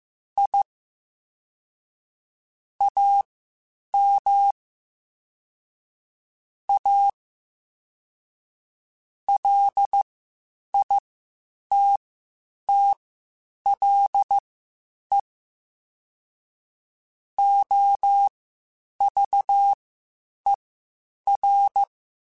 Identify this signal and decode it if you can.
Morse code, heard as a keyed tone, about 15 wpm, spaced as for 8 wpm, 800 Hz. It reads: I AM A LITTLE OVER